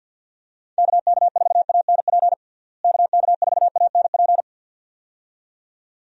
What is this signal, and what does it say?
Morse code, heard as a keyed tone, about 51 wpm, 700 Hz. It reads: KK4ANP KK4ANP
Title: USA Call Signs - 6 Characters - Repeated 2x - 50wpm